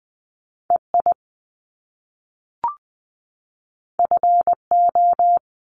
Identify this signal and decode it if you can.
Morse code, heard as a keyed tone, about 20 wpm, 700 Hz. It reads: EI FO
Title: Top Two Letter Combinations in Words 20wpm